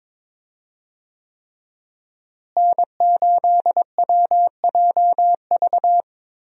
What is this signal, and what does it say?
Morse code, heard as a keyed tone, about 22 wpm, 700 Hz. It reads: N8WJV
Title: USA Call Signs - 5 Characters - Repeated 3x - 22wpm